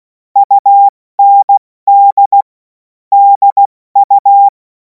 Morse code, heard as a keyed tone, about 16 wpm, 800 Hz. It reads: UND DU
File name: Morsekurs / Witze / Witz 30. Juni 2026 70 F80